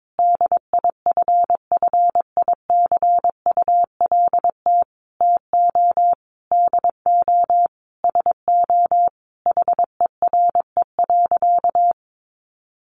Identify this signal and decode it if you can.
Morse code, heard as a keyed tone, about 22 wpm, 700 Hz. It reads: DIFFICULT TO DO SO HERE.
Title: Sentences from Top 600 Words 22wpm